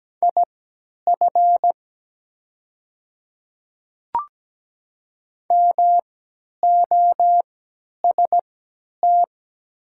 Morse code, heard as a keyed tone, about 17 wpm, spaced as for 10 wpm, 700 Hz. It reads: IF MOST
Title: 3x Character Spacing - Top 100 Words - 17wpm